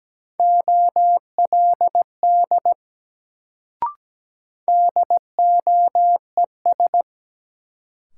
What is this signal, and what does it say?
Morse code, heard as a keyed tone, about 17 wpm, 700 Hz. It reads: OLD DOES
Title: Sets of 2 Words - New Word Drill - 17wpm